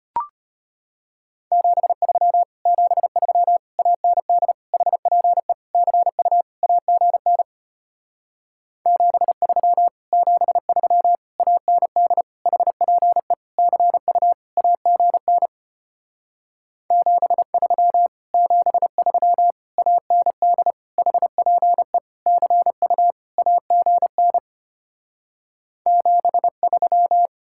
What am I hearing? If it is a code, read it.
Morse code, heard as a keyed tone, about 33 wpm, 700 Hz. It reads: 73 73 AND HPE CU AGN 73 73 AND HPE CU AGN 73 73 AND HPE CU AGN 73